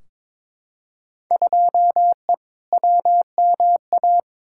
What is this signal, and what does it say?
Morse code, heard as a keyed tone, about 22 wpm, 700 Hz. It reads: 2E WMA
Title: Field Day Exchanges Encore - 22wpm